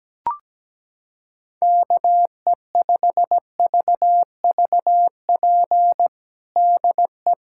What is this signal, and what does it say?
Morse code, heard as a keyed tone, about 17 wpm, 700 Hz. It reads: KE5VVP DE
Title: USA Callsign DE Callsign 17wpm